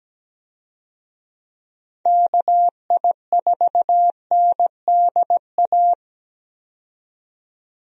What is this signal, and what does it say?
Morse code, heard as a keyed tone, about 17 wpm, 700 Hz. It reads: KI4NDA
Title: USA Call Signs - 6 Characters - 17wpm